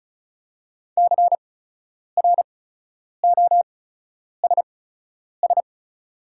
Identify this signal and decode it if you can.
Morse code, heard as a keyed tone, about 36 wpm, spaced as for 10 wpm, 690 Hz. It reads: CROSS
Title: 8x Character Spacing - Top 500 Words - 35wpm